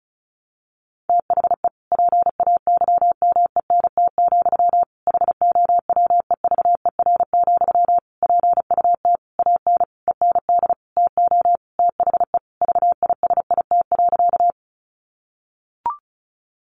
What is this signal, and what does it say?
Morse code, heard as a keyed tone, about 34 wpm, 700 Hz. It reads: THE PAYMENT, HOWEVER, PUT AN END TO THE VISIT.
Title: Sentences from Top 1200 Words 35wpm